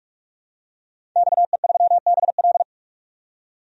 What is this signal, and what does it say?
Morse code, heard as a keyed tone, about 46 wpm, 700 Hz. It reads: XE3BL
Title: DXCC-100 Call Signs - 5 Characters - 45wpm